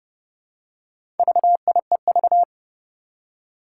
Morse code, heard as a keyed tone, about 30 wpm, 700 Hz. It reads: VIEV
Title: Four Challenging Letter-Number Combinations 30wpm